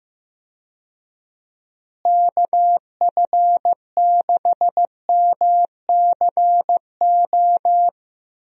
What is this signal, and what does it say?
Morse code, heard as a keyed tone, about 15 wpm, 700 Hz. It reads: KF6MCO
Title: USA Call Signs - 6 Characters - Repeated 3x - 15wpm